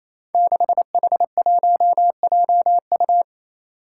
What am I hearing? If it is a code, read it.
Morse code, heard as a keyed tone, about 28 wpm, 700 Hz. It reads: 6H1JU
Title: Five Challenging Letter-Number Combinations 28wpm